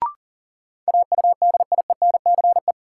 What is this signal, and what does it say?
Morse code, heard as a keyed tone, about 40 wpm, 700 Hz. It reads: AUDIENCE
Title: Sentences from Top 800 Words - New Word Drill - 40wpm